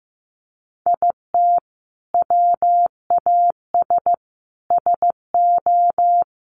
Morse code, heard as a keyed tone, about 14 wpm, 700 Hz. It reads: IT WAS SO
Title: Sentences from Top 500 Words 15wpm